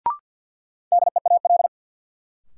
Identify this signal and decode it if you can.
Morse code, heard as a keyed tone, about 53 wpm, 700 Hz. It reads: DEAL
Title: Sets of 3 Words - New Word Drill - 50wpm